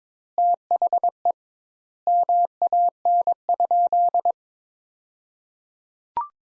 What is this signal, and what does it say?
Morse code, heard as a keyed tone, about 21 wpm, 700 Hz. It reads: THE MAN?
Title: Sentences Top 200 Words - 22wpm